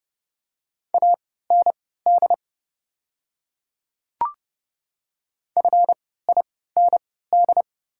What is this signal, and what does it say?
Morse code, heard as a keyed tone, about 30 wpm, spaced as for 17 wpm, 700 Hz. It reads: AND FIND
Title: 3x Character Spacing - Top 200 Words - 30wpm